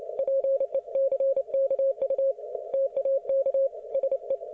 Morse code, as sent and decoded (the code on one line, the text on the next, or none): .--. -.-. -.- ..- / - .- -.- / ... .
PECKU TAK SE